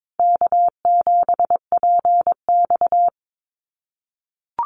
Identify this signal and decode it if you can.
Morse code, heard as a keyed tone, about 22 wpm, 700 Hz. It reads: K7PX
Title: NA SOTA Activator Callsigns 22wpm